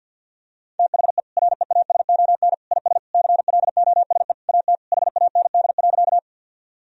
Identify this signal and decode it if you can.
Morse code, heard as a keyed tone, about 52 wpm, 700 Hz. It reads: THE REASON IS CLOSE AT HAND.
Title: Sentences from Top 1000 Words - Review Part 1 - 50wpm